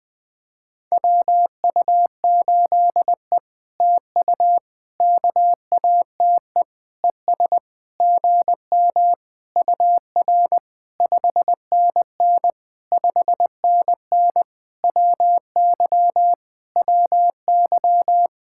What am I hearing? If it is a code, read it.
Morse code, heard as a keyed tone, about 20 wpm, 700 Hz. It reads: WU8E TU KATE ES GM UR 5NN 5NN WY WY
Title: POTA QSO Practice - 20wpm